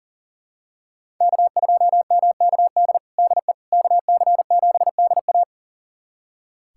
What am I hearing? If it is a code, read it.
Morse code, heard as a keyed tone, about 37 wpm, 700 Hz. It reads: K2MKD DE KC7DA
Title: USA Callsign DE Callsign - 5 Characters - 40wpm